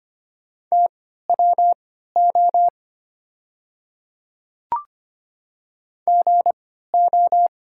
Morse code, heard as a keyed tone, about 25 wpm, spaced as for 14 wpm, 700 Hz. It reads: TWO GO
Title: 3x Character Spacing - Top 100 Words - 25wpm